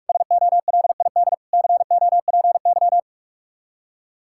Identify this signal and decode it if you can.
Morse code, heard as a keyed tone, about 49 wpm, 700 Hz. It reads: SOLID COPY